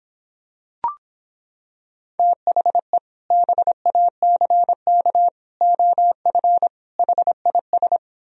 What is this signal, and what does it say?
Morse code, heard as a keyed tone, about 26 wpm, 700 Hz. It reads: THE BACK OF HIS